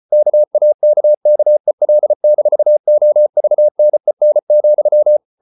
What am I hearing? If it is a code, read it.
Morse code, heard as a keyed tone, about 34 wpm, 600 Hz. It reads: KAKKEL-OVNEN,
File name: Ferie QRQ 34wpm